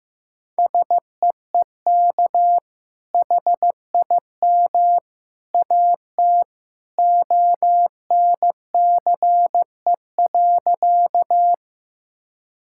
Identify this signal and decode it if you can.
Morse code, heard as a keyed tone, about 15 wpm, 700 Hz. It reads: SEEK HIM AT ONCE.
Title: Sentences from Top 1600 Words 15wpm